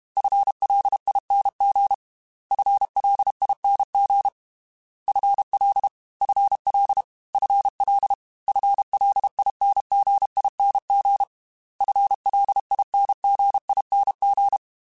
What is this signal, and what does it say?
Morse code, heard as a keyed tone, about 32 wpm, 800 Hz. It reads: FLING FLING FL FL FL FLINGING FLINGING
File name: FL / FL mp3s / FL Prefixes 3x2 Twice @30